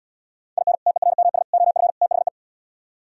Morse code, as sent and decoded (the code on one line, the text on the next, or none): .. -. / - . .-. -- ... / --- ..-. / - .... .
IN TERMS OF THE